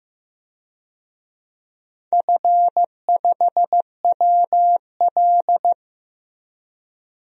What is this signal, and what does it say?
Morse code, heard as a keyed tone, about 16 wpm, 700 Hz. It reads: F5WL
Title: DXCC-100 Call Signs - 4 Characters - Repeated 2x - 15wpm